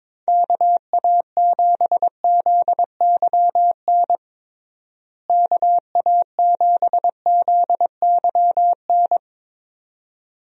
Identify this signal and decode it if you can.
Morse code, heard as a keyed tone, about 22 wpm, 700 Hz. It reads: KA7ZYN KA7ZYN
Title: USA Call Signs - 6 Characters - Repeated 2x - 22wpm